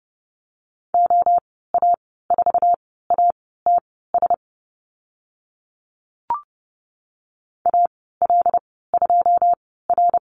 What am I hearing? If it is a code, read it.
Morse code, heard as a keyed tone, about 30 wpm, spaced as for 17 wpm, 700 Hz. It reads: OA4ATS AL2R
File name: Lesson 14g - Number 2 - Call Signs 30wpm